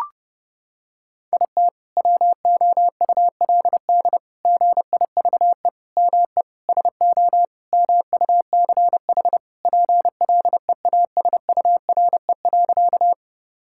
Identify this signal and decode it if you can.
Morse code, heard as a keyed tone, about 30 wpm, 700 Hz. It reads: IT WOULD GIVE ME SO MUCH PLEASURE.
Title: Sentences from Top 1600 Words 30wpm